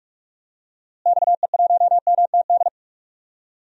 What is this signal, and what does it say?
Morse code, heard as a keyed tone, about 46 wpm, 700 Hz. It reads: XE1MTD
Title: DXCC-100 Call Signs - 6 Characters - Repeated 2x - 45wpm